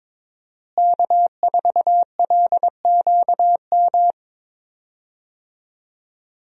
Morse code, heard as a keyed tone, about 22 wpm, 700 Hz. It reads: K4LQM
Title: USA Call Signs - 5 Characters - Repeated 3x - 22wpm